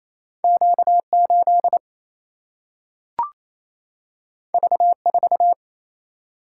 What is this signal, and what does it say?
Morse code, heard as a keyed tone, about 28 wpm, 700 Hz. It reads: Q 8 V 4
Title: Single-Letter Single-Number 28wpm